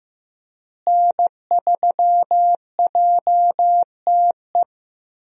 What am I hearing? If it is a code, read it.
Morse code, heard as a keyed tone, about 15 wpm, 700 Hz. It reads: N3JTE
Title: USA Call Signs - 5 Characters - Repeated 3x - 15wpm